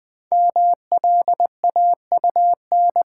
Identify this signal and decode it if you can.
Morse code, heard as a keyed tone, about 20 wpm, 700 Hz. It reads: MLAUN
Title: Code Groups - 5 Letters - 20wpm